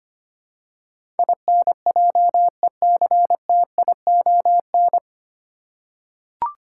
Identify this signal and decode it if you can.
Morse code, heard as a keyed tone, about 25 wpm, 700 Hz. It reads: INJECTION
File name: Nine Letter Words - 25wpm